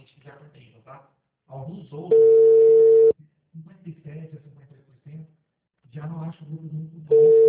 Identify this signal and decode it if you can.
Ringback tone (China, 450 Hz), with speech.